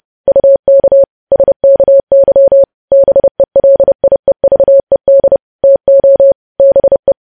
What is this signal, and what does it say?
Morse code, heard as a keyed tone, about 30 wpm, 550 Hz. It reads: UK SKY BELIEVED TO BE
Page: news headlines in morse code at 30 wpm